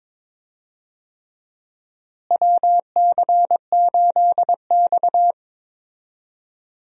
Morse code, heard as a keyed tone, about 22 wpm, 700 Hz. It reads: WC8X
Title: USA Call Signs - 4 Characters - Repeated 3x - 22wpm